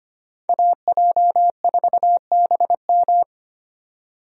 Morse code, heard as a keyed tone, about 25 wpm, 700 Hz. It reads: AJ4BM
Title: USA Call Signs - 5 Characters - 25wpm